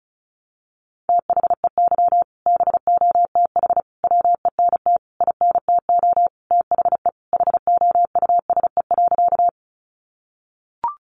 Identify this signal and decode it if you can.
Morse code, heard as a keyed tone, about 34 wpm, 700 Hz. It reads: THEY BOTH WENT INTO THE HOUSE.
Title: Sentences from Top 600 Words 35wpm